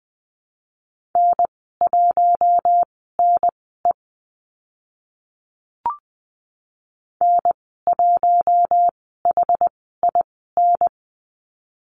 Morse code, heard as a keyed tone, about 20 wpm, spaced as for 14 wpm, 700 Hz. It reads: N1NE N1HIN
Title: 062 Lesson 11g - Letter h - Call Signs 20wpm